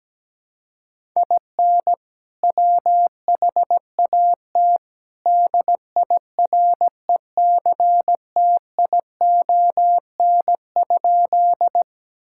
Morse code, heard as a keyed tone, about 17 wpm, 700 Hz. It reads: IN WHAT DIRECTION?
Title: Sentences from Top 1000 Words - Review Part 5 - 17wpm